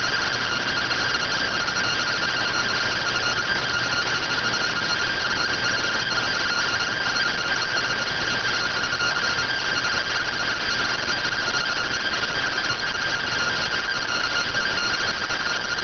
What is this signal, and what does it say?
Two-tone teleprinter signal, about 170 Hz shift.